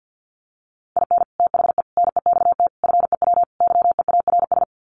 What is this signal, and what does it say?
Morse code, heard as a keyed tone, about 50 wpm, 700 Hz. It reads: IN THE NEXT FEW YEARS
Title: Sets of 5 Words Encore 50wpm